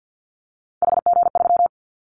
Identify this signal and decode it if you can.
Morse code, heard as a keyed tone, about 50 wpm, 700 Hz. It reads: HG3
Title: Callsign Prefixes for Top 50 DXCC - 50wpm